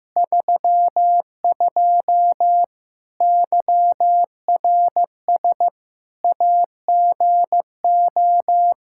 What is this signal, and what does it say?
Morse code, heard as a keyed tone, about 15 wpm, 700 Hz. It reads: 32 YRS AGO
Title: Practice QSO - Exchange 3 - More Info - 15wpm